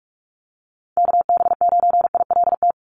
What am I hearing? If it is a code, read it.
Morse code, heard as a keyed tone, about 46 wpm, 690 Hz. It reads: KB9ILT
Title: NA SOTA Chaser Callsigns 45wpm